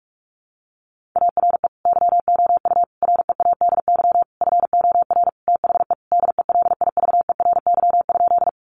Morse code, heard as a keyed tone, about 44 wpm, 700 Hz. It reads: ARE YOU READY FOR THE DELIVERY?
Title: Sentences from Top 1500 Words 45wpm